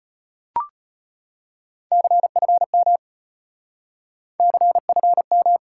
Morse code, heard as a keyed tone, about 33 wpm, 700 Hz. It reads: CFM CFM